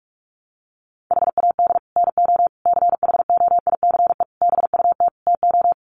Morse code, heard as a keyed tone, about 44 wpm, 700 Hz. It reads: HAD NO CHOICE BUT TO
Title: Sets of 5 Words 45wpm